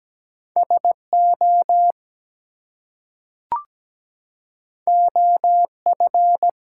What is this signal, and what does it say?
Morse code, heard as a keyed tone, about 17 wpm, 700 Hz. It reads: SO OF
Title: Two Letter Words - 17wpm